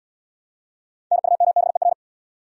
Morse code, heard as a keyed tone, about 76 wpm, 700 Hz. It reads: NUMBER